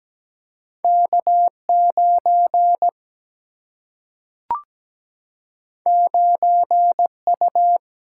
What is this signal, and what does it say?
Morse code, heard as a keyed tone, about 17 wpm, 700 Hz. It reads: K9 9U
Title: Single-Letter Single-Number 17wpm